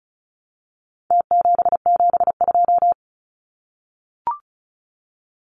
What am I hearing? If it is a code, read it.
Morse code, heard as a keyed tone, about 34 wpm, 700 Hz. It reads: T772
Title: Numbers - Contest Style - 4-Digit 35wpm